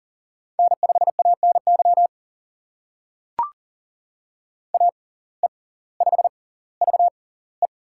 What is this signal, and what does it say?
Morse code, heard as a keyed tone, about 39 wpm, spaced as for 20 wpm, 690 Hz. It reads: N5ANY AE5VE
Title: USA Call Signs - 5 Characters - 6x Character Spacing - 40wpm